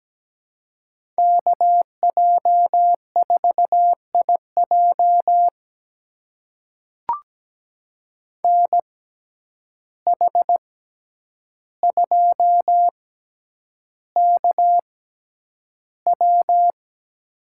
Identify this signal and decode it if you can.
Morse code, heard as a keyed tone, about 17 wpm, spaced as for 9 wpm, 700 Hz. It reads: KJ4IJ NH2KW
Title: USA Call Signs - 5 Characters - 6x Character Spacing - 17wpm